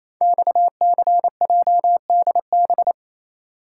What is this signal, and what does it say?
Morse code, heard as a keyed tone, about 28 wpm, 700 Hz. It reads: XCJDB
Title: Code Groups - 5 Letters - 28wpm